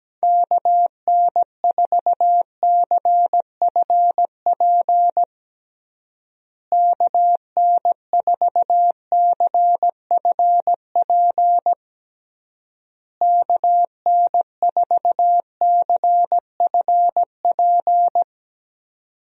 Morse code, heard as a keyed tone, about 17 wpm, 700 Hz. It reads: KN4CFP KN4CFP KN4CFP
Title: USA Call Signs - 6 Characters - Repeated 3x - 17wpm